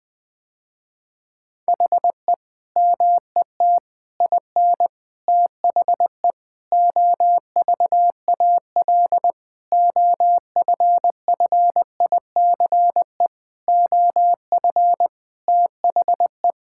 Morse code, heard as a keyed tone, about 20 wpm, 700 Hz. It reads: HE MET IN THE OVAL OFFICE OF THE
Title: Sentences with 8-Word Phrases - Spoken Beginning - 20wpm